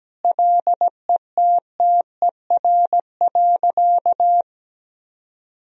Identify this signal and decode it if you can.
Morse code, heard as a keyed tone, about 17 wpm, 700 Hz. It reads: LETTER.
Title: Sentences from Top 1000 Words - Review Part 3 - 17wpm